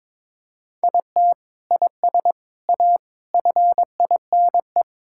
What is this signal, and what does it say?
Morse code, heard as a keyed tone, about 22 wpm, 700 Hz. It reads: IT IS A FINE
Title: Sentences from Top 1600 Words 22wpm